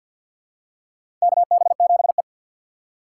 Morse code, heard as a keyed tone, about 52 wpm, 700 Hz. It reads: KB7E
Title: USA Call Signs - 4 Characters - 2x Character Spacing - 50wpm